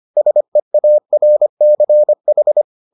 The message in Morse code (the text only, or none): SEARCH